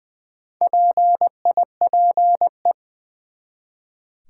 Morse code, heard as a keyed tone, about 20 wpm, 700 Hz. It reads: PIPE
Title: Four Letter Words - 20wpm